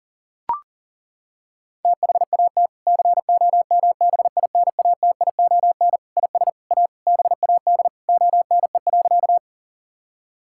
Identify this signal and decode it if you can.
Morse code, heard as a keyed tone, about 40 wpm, 700 Hz. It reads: THAT COMBINATION IS A BAD ONE.
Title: Sentences from Top 1000 Words - Review Part 6 - 40wpm